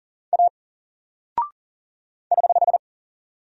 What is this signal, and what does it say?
Morse code, heard as a keyed tone, about 39 wpm, 700 Hz. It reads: A <HH>
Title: Single Character (Letter:Number:Punctuation:Prosign) 40wpm